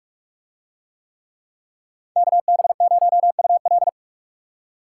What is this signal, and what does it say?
Morse code, heard as a keyed tone, about 40 wpm, 700 Hz. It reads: KB0UL